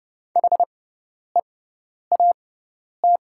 Morse code, heard as a keyed tone, about 30 wpm, spaced as for 10 wpm, 700 Hz. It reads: HEAT
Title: 6x Character Spacing - Top 500 Words - 30wpm